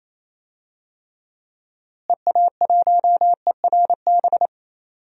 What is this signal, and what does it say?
Morse code, heard as a keyed tone, about 28 wpm, 700 Hz. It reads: EA1ERB
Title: DXCC-100 Call Signs - 6 Characters - Repeated 3x - 28wpm